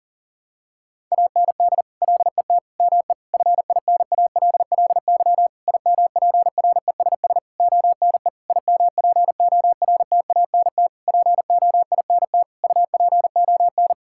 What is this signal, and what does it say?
Morse code, heard as a keyed tone, about 40 wpm, 700 Hz. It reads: AND LET ME FINALLY IMPRESS ONE IMPORTANT POINT UPON
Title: Sentences from Top 1600 Words 40wpm